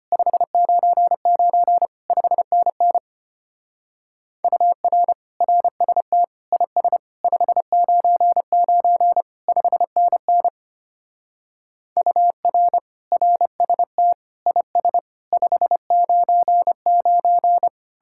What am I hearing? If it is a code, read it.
Morse code, heard as a keyed tone, about 32 wpm, 700 Hz. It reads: 599 5NN UR RST IS 599 5NN UR RST IS 599